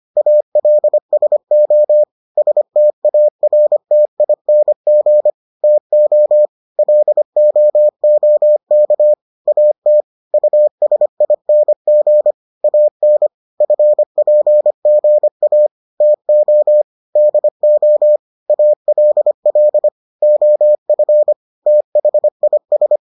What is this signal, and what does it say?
Morse code, heard as a keyed tone, about 25 wpm, 600 Hz. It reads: ALSO STARTING TO LOOK AT USING AN FPGA TO DO ALL OF THIS